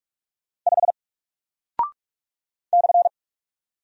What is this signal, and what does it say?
Morse code, heard as a keyed tone, about 44 wpm, 700 Hz. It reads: 5 /
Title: Single Character (Letter:Number:Punctuation:Prosign) Alt 45wpm